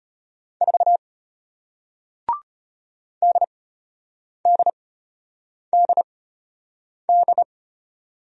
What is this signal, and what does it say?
Morse code, heard as a keyed tone, about 33 wpm, spaced as for 8 wpm, 700 Hz. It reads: 4 DDDD